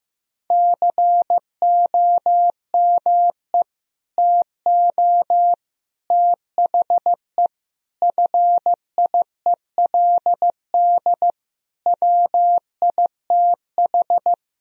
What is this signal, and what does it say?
Morse code, heard as a keyed tone, about 16 wpm, 700 Hz. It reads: COME TO THE FIELD WITH
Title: Sentences from Top 1000 Words - Review Part 2 - 15wpm